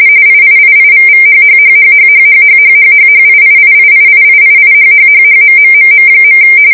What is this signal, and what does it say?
RTTY, about 45.45 baud. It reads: IIIIVVHHHHHHHHHHHHIIIIIIIIIHHHHMMMM.